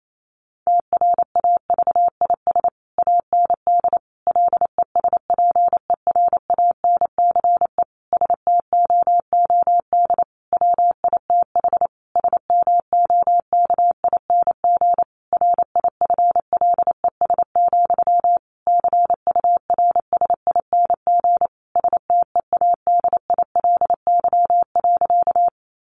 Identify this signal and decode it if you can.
Morse code, heard as a keyed tone, about 28 wpm, 700 Hz. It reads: TRAVIS AND LESPERANCE STOOD WITH SMOKING RIFLES, CURSING STEADILY.
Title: A Sound Of Thunder 28wpm